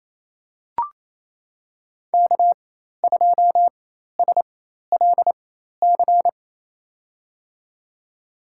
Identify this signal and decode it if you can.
Morse code, heard as a keyed tone, about 28 wpm, spaced as for 13 wpm, 700 Hz. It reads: K2SLC
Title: USA Call Signs - 5 Characters - 4x Character Spacing - 28wpm